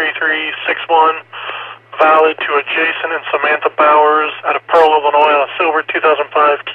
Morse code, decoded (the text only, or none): none